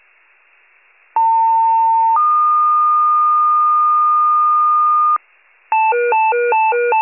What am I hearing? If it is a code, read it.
Two-tone page, heard as a one-second tone, then a three-second tone, about 900 then 1220 Hz. It is heard from 1.2 to 5.2 s.